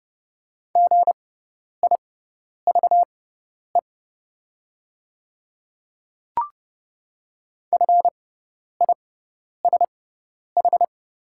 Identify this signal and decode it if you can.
Morse code, heard as a keyed tone, about 31 wpm, spaced as for 10 wpm, 700 Hz. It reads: GIVE FISH
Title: 6x Character Spacing - Top 300 Words - 30wpm